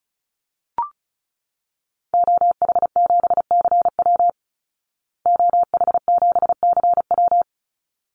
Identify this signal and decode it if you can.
Morse code, heard as a keyed tone, about 34 wpm, 700 Hz. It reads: OH7CW OH7CW
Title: Call Signs Repeated 3x - 35wpm